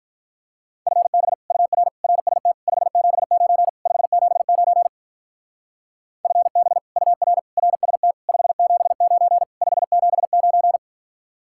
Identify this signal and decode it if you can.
Morse code, heard as a keyed tone, about 44 wpm, 700 Hz. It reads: FB UR RST 579 579 FB UR RST 579 579